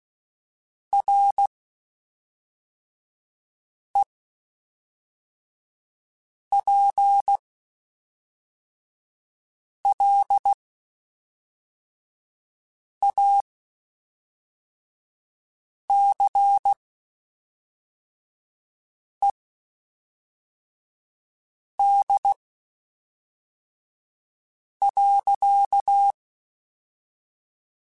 Morse code, heard as a keyed tone, about 16 wpm, spaced as for 3 wpm, 800 Hz. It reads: REPLACED.
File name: Spaced